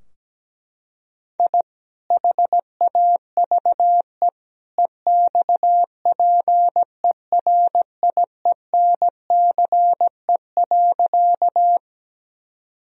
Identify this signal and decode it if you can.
Morse code, heard as a keyed tone, about 17 wpm, 700 Hz. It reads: I HAVE EXPERIENCE.
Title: Sentences from Top 1000 Words - Review Part 3 - 17wpm